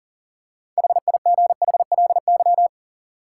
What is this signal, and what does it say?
Morse code, heard as a keyed tone, about 37 wpm, 700 Hz. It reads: HIGHLY
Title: Six Letter Words - 40wpm